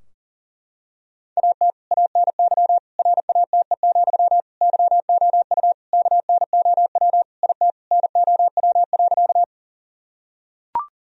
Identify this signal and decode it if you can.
Morse code, heard as a keyed tone, about 37 wpm, 700 Hz. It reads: AT ANY RATE, YOU KNOW IT NOW.
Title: Sentences from Top 1000 Words - Review Part 4 - 40wpm